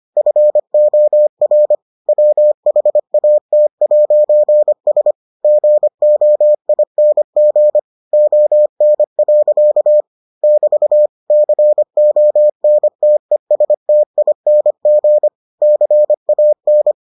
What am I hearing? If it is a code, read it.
Morse code, heard as a keyed tone, about 25 wpm, 600 Hz. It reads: FOR WHAT'S GOING ON. <BT> CONTESTING CAN